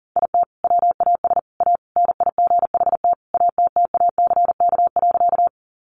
Morse code, heard as a keyed tone, about 40 wpm, 700 Hz. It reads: IT WAS A NIGHT ATTACK.
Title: Sentences from Top 1500 Words 40wpm